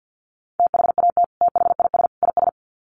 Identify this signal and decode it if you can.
Morse code, heard as a keyed tone, about 50 wpm, 700 Hz. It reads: THAT THIS IS
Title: Sets of 3 Words 50wpm